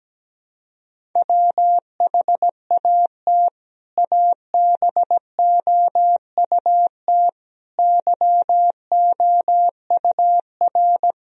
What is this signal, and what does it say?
Morse code, heard as a keyed tone, about 17 wpm, 700 Hz. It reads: WHAT ABOUT YOUR
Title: Sentences from Top 1000 Words - Review Part 1 - 17wpm